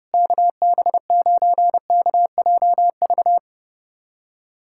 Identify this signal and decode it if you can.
Morse code, heard as a keyed tone, about 30 wpm, 700 Hz. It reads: KB9KJV
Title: USA Callsign DE Callsign - 6 Characters - 30wpm